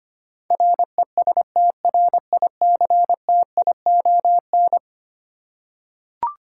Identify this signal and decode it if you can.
Morse code, heard as a keyed tone, about 25 wpm, 700 Hz. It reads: RESTRICTION
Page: morse-code-ninja-practice-25wpm